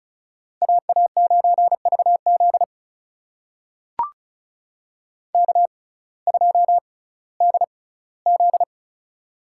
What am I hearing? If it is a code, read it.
Morse code, heard as a keyed tone, about 36 wpm, spaced as for 19 wpm, 700 Hz. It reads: AA9VZ K2DZ
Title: USA Call Signs - 6x Character Spacing - 35wpm